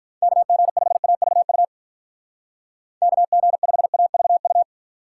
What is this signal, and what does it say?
Morse code, heard as a keyed tone, about 45 wpm, 700 Hz. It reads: KG5AVU KG5AVU